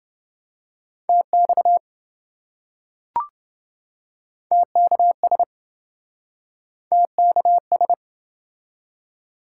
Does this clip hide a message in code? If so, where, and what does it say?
Yes: Morse code from 1.1 to 7.9 s, heard as a keyed tone, about 27 wpm, 700 Hz. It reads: TX TKS TKS